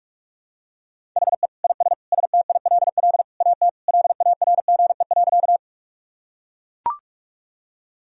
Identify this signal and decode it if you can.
Morse code, heard as a keyed tone, about 42 wpm, 700 Hz. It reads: HE IS STILL AT LARGE.
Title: Sentences from Top 1000 Words - Review Part 1 - 45wpm